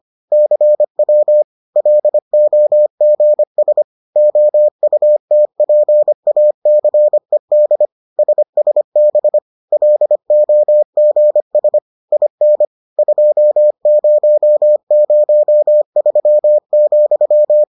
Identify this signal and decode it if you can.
Morse code, heard as a keyed tone, about 25 wpm, 600 Hz. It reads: CW LOGS OUTPACED SSB LOGS IN 2003,